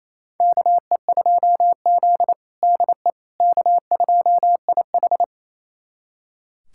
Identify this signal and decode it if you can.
Morse code, heard as a keyed tone, about 28 wpm, 700 Hz. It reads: KE2Z DE K2IH
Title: USA Callsign DE Callsign - 4 Characters - 28wpm